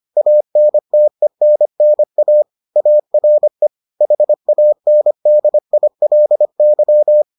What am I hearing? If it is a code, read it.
Morse code, heard as a keyed tone, about 25 wpm, 600 Hz. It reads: ANTENNA ARE HANDILY